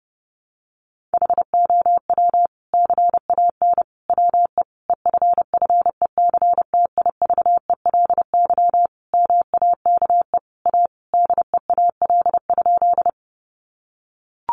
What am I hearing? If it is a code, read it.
Morse code, heard as a keyed tone, about 30 wpm, 700 Hz. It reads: HOW CAN WE EFFECTIVELY MAKE A DEAL?
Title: Sentences from Top 1400 Words 30wpm